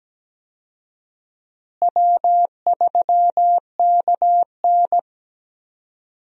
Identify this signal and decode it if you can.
Morse code, heard as a keyed tone, about 17 wpm, 700 Hz. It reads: W3KN
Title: USA Call Signs - 4 Characters - Repeated 2x - 17wpm